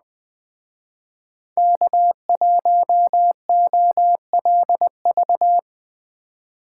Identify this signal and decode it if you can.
Morse code, heard as a keyed tone, about 20 wpm, 700 Hz. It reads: K1OLV
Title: Speed Racing - USA 1x3 Call Signs 17wpm